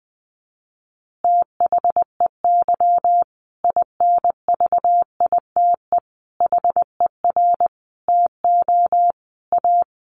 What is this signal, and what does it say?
Morse code, heard as a keyed tone, about 20 wpm, 700 Hz. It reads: THEY INVITE HER TO A
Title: Easy Phrases 20wpm